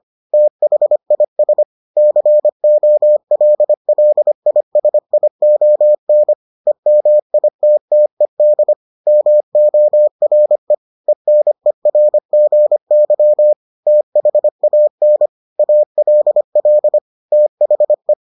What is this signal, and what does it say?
Morse code, heard as a keyed tone, about 25 wpm, 600 Hz. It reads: THIS COLLISION EMITTED MORE ENERGY THAN ALL THE